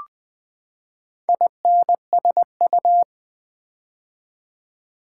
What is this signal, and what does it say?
Morse code, heard as a keyed tone, about 20 wpm, 700 Hz. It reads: INSU
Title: Build A Word - 4 Letters At A Time - 12-Letter Words - 20wpm